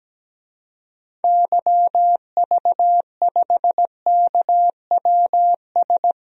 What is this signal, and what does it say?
Morse code, heard as a keyed tone, about 17 wpm, 700 Hz. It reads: YV5KWS
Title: DXCC-100 Call Signs - 6 Characters - 17wpm